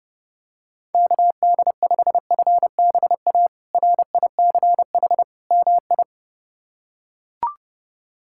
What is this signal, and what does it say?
Morse code, heard as a keyed tone, about 30 wpm, 700 Hz. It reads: KD5FBA RICH MI
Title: K1USN SST QSO Practice - 30wpm